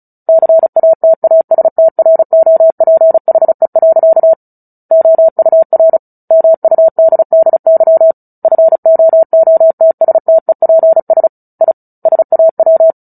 Morse code, heard as a keyed tone, about 34 wpm, 650 Hz. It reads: CATASTROPHE. OUR MUDDY FOOTSTEPS I SAW
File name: Book Two / 09-Chapter 9 - 35 WPM